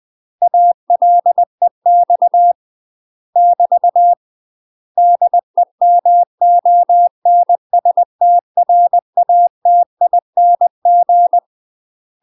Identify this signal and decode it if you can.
Morse code, heard as a keyed tone, about 20 wpm, 700 Hz. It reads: ALEX <BT> DEMONSTRATING
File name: UK MASTER QSOs TRAINING / 20WPM / UK MASTER QSOs 0041